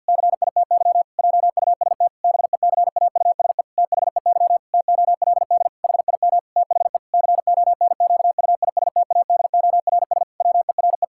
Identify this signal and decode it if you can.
Morse code, heard as a keyed tone, about 50 wpm, 700 Hz. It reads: CITY JUST BECAUSE THEY TOLD HIM THE CONQUISTADORS WERE
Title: Pangram Challenge v2 - 50wpm